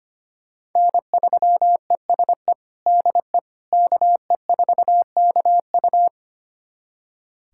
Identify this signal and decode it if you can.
Morse code, heard as a keyed tone, about 25 wpm, 700 Hz. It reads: N3ESE DE KE4KU
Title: USA Callsign DE Callsign - 5 Characters - 25wpm